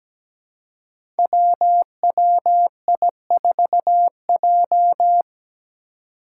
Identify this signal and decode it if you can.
Morse code, heard as a keyed tone, about 17 wpm, 700 Hz. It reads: WWI4J
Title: Five Challenging Letter-Number Combinations 17wpm